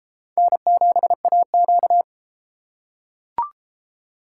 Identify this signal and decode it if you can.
Morse code, heard as a keyed tone, about 33 wpm, 700 Hz. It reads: N7AQ